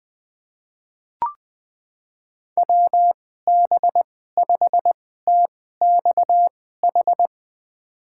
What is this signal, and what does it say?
Morse code, heard as a keyed tone, about 21 wpm, spaced as for 15 wpm, 700 Hz. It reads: WB5TXH
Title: USA Call Signs - 2x Character Spacing - 20wpm